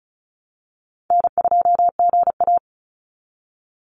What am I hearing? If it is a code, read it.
Morse code, heard as a keyed tone, about 34 wpm, 700 Hz. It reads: N2GA
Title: Easy Call Signs Repeated 2x - 35wpm